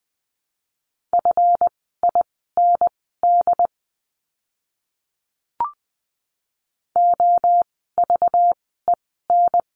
Morse code, heard as a keyed tone, about 20 wpm, spaced as for 14 wpm, 700 Hz. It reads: FIND OVEN
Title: 192 Lesson 31f - Letter v - Words 20wpm